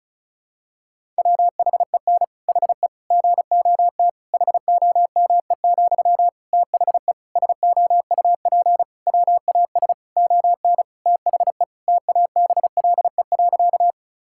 Morse code, heard as a keyed tone, about 34 wpm, 700 Hz. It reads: WHEN HE GOT HOME, THE SOUP WAS ON THE TABLE.
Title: Sentences from Top 1000 Words - Review Part 6 - 35wpm